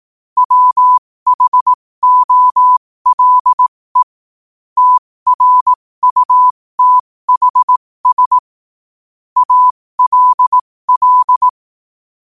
Morse code, heard as a keyed tone, about 18 wpm, spaced as for 15 wpm, 1000 Hz. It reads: WHOLE TRUTHS ALL
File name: qotdfiles / 15wpm